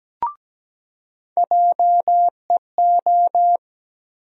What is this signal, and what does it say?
Morse code, heard as a keyed tone, about 17 wpm, 700 Hz. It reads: JEO
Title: USA Call Sign Postfixes - 3 Letters 17wpm